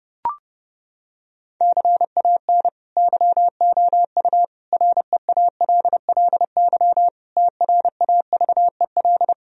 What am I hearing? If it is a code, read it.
Morse code, heard as a keyed tone, about 30 wpm, 700 Hz. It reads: CAN YOU REALLY TRAVEL
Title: Sentences from Top 1000 Words - Review Part 2 - 30wpm